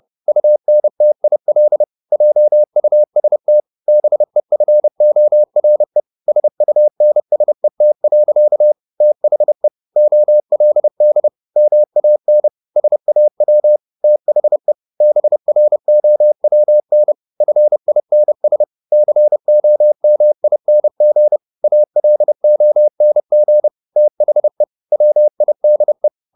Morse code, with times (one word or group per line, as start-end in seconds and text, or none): 0.3-1.8 UNTIL
2.1-3.6 JUST
3.9-6.0 BEFORE
6.3-8.7 SUNSET.
9.0-9.7 THE
10.0-11.3 OLD
11.6-12.5 MAN
12.8-13.8 SAW
14.0-14.7 THE
15.0-17.1 BROWN
17.4-18.6 FINS
18.9-21.4 COMING
21.6-23.7 ALONG
24.0-24.6 THE
24.9-26.1 WIDE